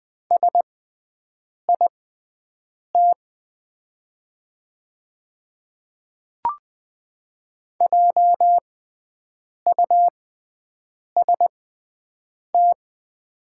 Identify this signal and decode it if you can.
Morse code, heard as a keyed tone, about 20 wpm, spaced as for 7 wpm, 700 Hz. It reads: SIT JUST
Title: 6x Character Spacing - Top 500 Words - 20wpm